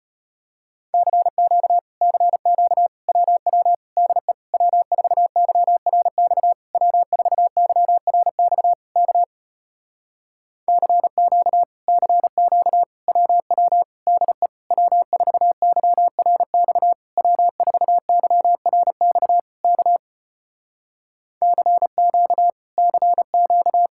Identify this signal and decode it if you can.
Morse code, heard as a keyed tone, about 33 wpm, 700 Hz. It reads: CQ CQ WW DE W4YRX W4YRX K CQ CQ WW DE W4YRX W4YRX K CQ CQ